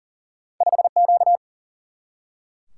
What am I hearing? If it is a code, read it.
Morse code, heard as a keyed tone, about 40 wpm, 700 Hz. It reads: 5 Q
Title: Single-Letter Single-Number 40wpm